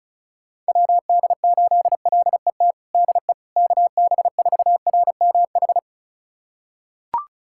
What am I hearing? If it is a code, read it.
Morse code, heard as a keyed tone, about 34 wpm, 690 Hz. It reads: WD8LET DE KB4RMH
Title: USA Callsign DE Callsign 35wpm